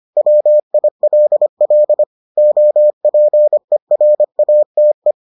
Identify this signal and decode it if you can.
Morse code, heard as a keyed tone, about 25 wpm, 600 Hz. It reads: WILL OPERATE